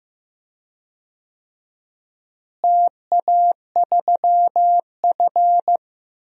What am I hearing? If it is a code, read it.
Morse code, heard as a keyed tone, about 15 wpm, 700 Hz. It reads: TA3F
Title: DXCC-100 Call Signs - 4 Characters - Repeated 2x - 15wpm